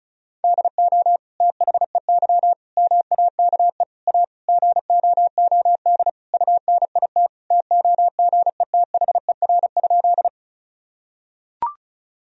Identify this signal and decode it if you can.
Morse code, heard as a keyed tone, about 34 wpm, 700 Hz. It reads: DO THEY MAKE A GOOD UNIT TOGETHER?
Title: Sentences from Top 1000 Words - Review Part 2 - 35wpm